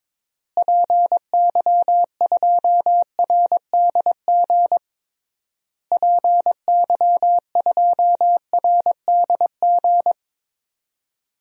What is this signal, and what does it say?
Morse code, heard as a keyed tone, about 22 wpm, 700 Hz. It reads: PY2RDG PY2RDG
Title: DXCC-100 Call Signs - 6 Characters - Repeated 3x - 22wpm